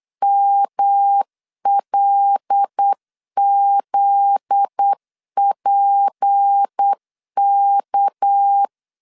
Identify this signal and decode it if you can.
Morse code, heard as a keyed tone, about 8 wpm, 800 Hz. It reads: MLZPK
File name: LEZIONI / 8 YJ45 / 8 Y 8wpm 3245